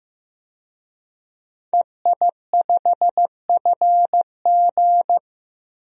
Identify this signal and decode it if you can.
Morse code, heard as a keyed tone, about 15 wpm, 700 Hz. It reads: EI5FG
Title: DXCC-100 Call Signs - 5 Characters - 15wpm